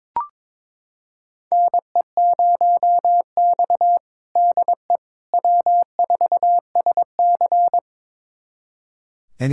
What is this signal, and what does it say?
Morse code, heard as a keyed tone, about 22 wpm, 700 Hz. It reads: NE0X DE W4SC
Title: USA Callsign DE Callsign - 4 Characters - 22wpm